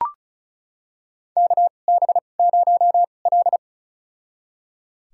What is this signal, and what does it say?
Morse code, heard as a keyed tone, about 36 wpm, spaced as for 25 wpm, 700 Hz. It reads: KB0L
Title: USA Call Signs - 4 Characters - 2x Character Spacing - 35wpm